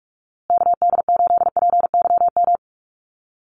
Morse code, heard as a keyed tone, about 44 wpm, 700 Hz. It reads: KD8PYM
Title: USA Call Signs Repeated 2x 45wpm